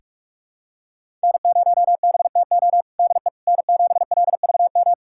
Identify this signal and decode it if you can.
Morse code, heard as a keyed tone, about 46 wpm, 700 Hz. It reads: N0BTO DE N7LVM
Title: USA Callsign DE Callsign - 5 Characters - 45wpm